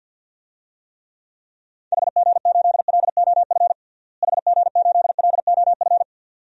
Morse code, heard as a keyed tone, about 50 wpm, 700 Hz. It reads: HG8LOF HG8LOF
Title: DXCC-100 Call Signs - 6 Characters - Repeated 2x - 50wpm